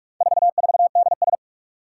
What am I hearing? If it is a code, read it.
Morse code, heard as a keyed tone, about 46 wpm, 700 Hz. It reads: 44DS
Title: Four Challenging Letter-Number Combinations 45wpm